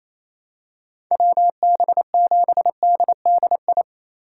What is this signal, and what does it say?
Morse code, heard as a keyed tone, about 28 wpm, 700 Hz. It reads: WB7DDI
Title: USA Call Signs - 6 Characters - 8x Character Spacing - 28wpm